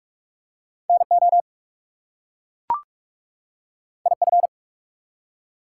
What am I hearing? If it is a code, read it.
Morse code, heard as a keyed tone, about 44 wpm, 700 Hz. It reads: NO IF
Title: Two Letter Words - 45wpm